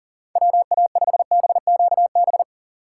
Morse code, heard as a keyed tone, about 40 wpm, 700 Hz. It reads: WA5BQB
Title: USA Call Signs - 6 Characters - 4x Character Spacing - 40wpm